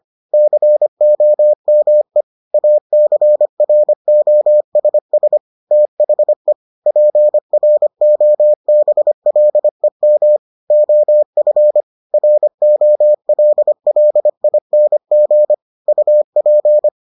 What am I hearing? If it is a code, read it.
Morse code, heard as a keyed tone, about 25 wpm, 600 Hz. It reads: COME ACROSS THE PROBLEM OF ROLLING UP